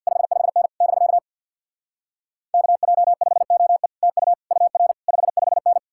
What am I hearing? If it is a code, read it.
Morse code, heard as a keyed tone, about 64 wpm, 700 Hz. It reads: 55N <BK> KJ5OE TU UR 55N